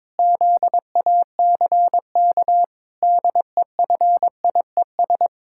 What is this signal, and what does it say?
Morse code, heard as a keyed tone, about 22 wpm, 700 Hz. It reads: ZACK DEFIES
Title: Pangram Challenge v2 - 22wpm